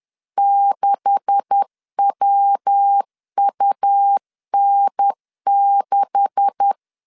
Morse code, heard as a keyed tone, about 10 wpm, 800 Hz. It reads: 6WUN6
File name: LEZIONI / 9 286 10wpm 2735